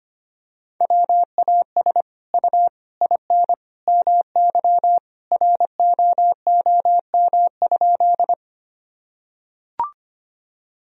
Morse code, heard as a keyed tone, about 25 wpm, 700 Hz. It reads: WAS U IN MY ROOM?
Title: Sentences from Top 300 Words and Abbreviations 25wpm